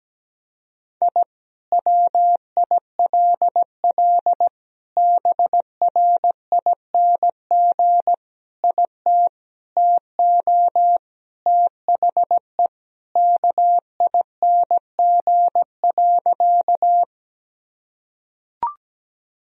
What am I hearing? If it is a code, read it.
Morse code, heard as a keyed tone, about 17 wpm, 700 Hz. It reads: I WILL BRING IT TO THE KING.
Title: Sentences from Top 1000 Words - Review Part 3 - 17wpm